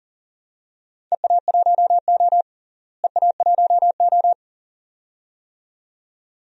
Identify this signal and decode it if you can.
Morse code, heard as a keyed tone, about 40 wpm, 700 Hz. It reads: EA1O EA1O
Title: DXCC-100 Call Signs - 4 Characters - Repeated 2x - 40wpm